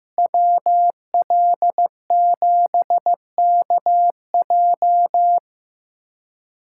Morse code, heard as a keyed tone, about 15 wpm, 700 Hz. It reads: WL7KJ
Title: USA Callsign DE Callsign - 5 Characters - 15wpm